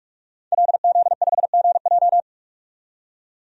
Morse code, heard as a keyed tone, about 46 wpm, 700 Hz. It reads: LZ5GJ
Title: DXCC-100 Call Signs - 5 Characters - 45wpm